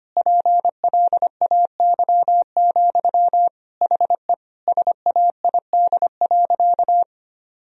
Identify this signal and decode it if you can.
Morse code, heard as a keyed tone, about 25 wpm, 700 Hz. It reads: PLAY, HE SAID.
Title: Sentences from Top 1000 Words - Review Part 4 - 25wpm